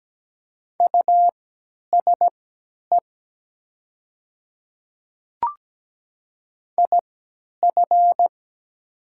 Morse code, heard as a keyed tone, about 17 wpm, spaced as for 10 wpm, 700 Hz. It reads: USE IF
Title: 3x Character Spacing - Top 100 Words - 17wpm